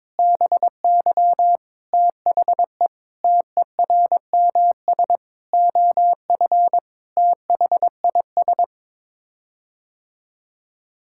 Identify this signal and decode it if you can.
Morse code, heard as a keyed tone, about 22 wpm, 700 Hz. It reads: BY THE TERMS OF THIS
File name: Sets of 9 Words - 22wpm